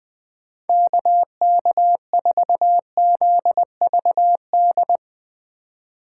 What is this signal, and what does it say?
Morse code, heard as a keyed tone, about 20 wpm, 700 Hz. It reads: KK4ZVD